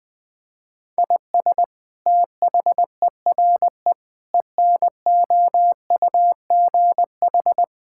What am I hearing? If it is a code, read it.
Morse code, heard as a keyed tone, about 20 wpm, 700 Hz. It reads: IS THERE ENOUGH
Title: Sentences from Top 1000 Words - Review Part 5 - 20wpm